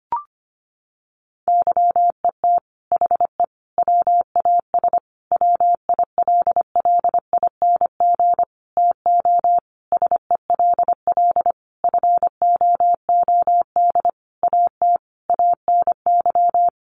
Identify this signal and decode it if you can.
Morse code, heard as a keyed tone, about 25 wpm, 700 Hz. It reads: YET HE WAS WILLING TO SELL FOOD AT ANY
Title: Sentences from Top 1100 Words 25wpm